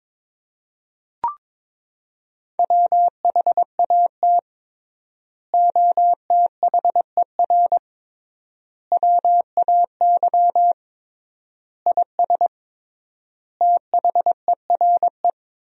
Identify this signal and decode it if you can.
Morse code, heard as a keyed tone, about 22 wpm, 700 Hz. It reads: WHAT OTHER WAY IS THERE
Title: 3x Word-Spacing - Sentences from Top 100 Words - 22wpm